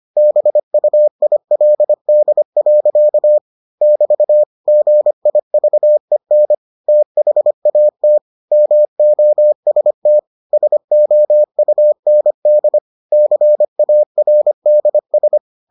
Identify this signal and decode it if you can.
Morse code, heard as a keyed tone, about 25 wpm, 600 Hz. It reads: BUILD. <BT> GIVEN THAT MOST SOUND CARDS